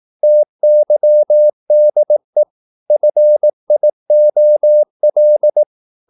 Morse code, heard as a keyed tone, about 18 wpm, 600 Hz. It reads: TYDE FIOL
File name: Korte ord DK 18wpm